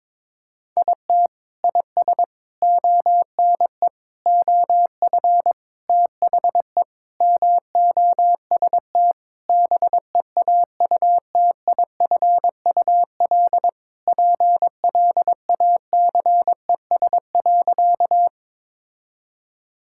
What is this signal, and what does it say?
Morse code, heard as a keyed tone, about 21 wpm, 700 Hz. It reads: IT IS ONE OF THE MOST BEAUTIFUL PLACES.
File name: practice / 22wpm / Sentences with 3-Word Phrases - 22wpm